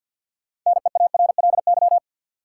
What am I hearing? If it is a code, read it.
Morse code, heard as a keyed tone, about 53 wpm, 700 Hz. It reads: NEARLY
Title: Sentences from Top 1300 Words - New Word Drill Alt - 50wpm